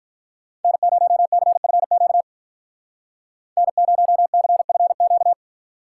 Morse code, heard as a keyed tone, about 44 wpm, 700 Hz. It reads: N0CFQ N0CFQ